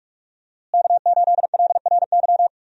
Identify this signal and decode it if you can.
Morse code, heard as a keyed tone, about 46 wpm, 700 Hz. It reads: K8LRY
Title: USA Call Signs - 5 Characters - Repeated 3x - 45wpm